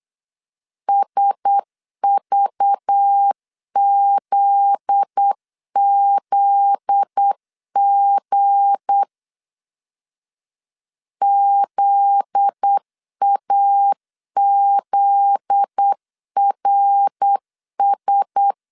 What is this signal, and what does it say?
Morse code, heard as a keyed tone, about 8 wpm, 800 Hz. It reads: SVZZG ZAZRS
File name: LEZIONI / 7 GF9Z / 7 GF9Z 8wpm 3213